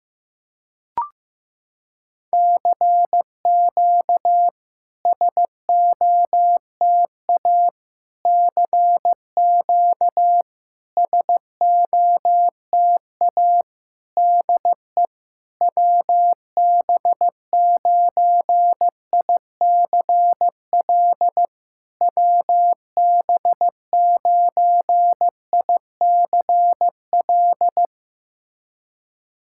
Morse code, heard as a keyed tone, about 15 wpm, 700 Hz. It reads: CQ SOTA CQ SOTA DE WB9ICL WB9ICL
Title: SOTA QSO Practice 15wpm